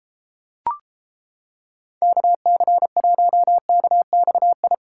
Morse code, heard as a keyed tone, about 31 wpm, 700 Hz. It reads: KC1KXI